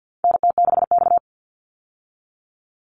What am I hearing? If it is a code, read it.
Morse code, heard as a keyed tone, about 50 wpm, 700 Hz. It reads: NT6X
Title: Call Signs Repeated 3x - 50wpm